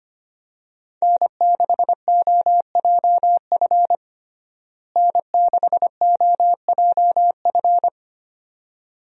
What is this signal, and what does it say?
Morse code, heard as a keyed tone, about 25 wpm, 700 Hz. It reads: N6OJF N6OJF
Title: USA Call Signs - 5 Characters - Repeated 2x - 25wpm